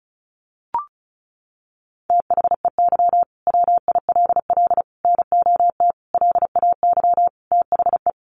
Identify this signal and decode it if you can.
Morse code, heard as a keyed tone, about 34 wpm, 700 Hz. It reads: THEY WILL NOT LAY THE
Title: Sentences from Top 1500 Words 35wpm